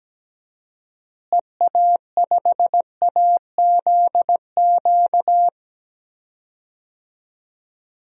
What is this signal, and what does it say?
Morse code, heard as a keyed tone, about 17 wpm, 700 Hz. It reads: EA5AZQ
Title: DXCC-100 Call Signs - 6 Characters - Repeated 3x - 17wpm